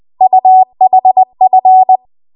Morse code, heard as a keyed tone, about 20 wpm, 750 Hz. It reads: UHF
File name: Archive / 20 WPM / 20WPM